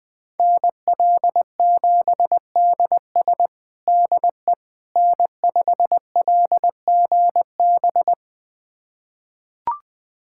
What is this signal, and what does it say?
Morse code, heard as a keyed tone, about 21 wpm, 700 Hz. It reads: NL7DS DE N5LGB
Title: USA Callsign DE Callsign - 5 Characters - 20wpm